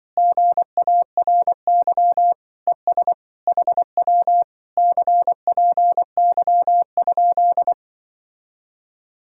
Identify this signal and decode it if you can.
Morse code, heard as a keyed tone, about 24 wpm, 700 Hz. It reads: GARY ES HW CPY?
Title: Speed-Racing Common QSO Phrases - 20wpm